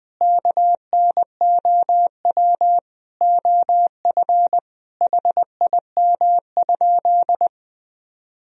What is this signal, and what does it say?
Morse code, heard as a keyed tone, about 20 wpm, 700 Hz. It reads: KNOW OF HIM?
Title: Sentences from Top 100 Words and Abbreviations 20wpm